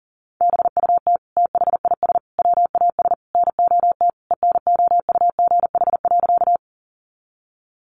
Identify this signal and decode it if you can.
Morse code, heard as a keyed tone, about 40 wpm, 700 Hz. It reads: BUT THIS WAS NOT ENOUGH.
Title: Sentences from Top 600 Words 40wpm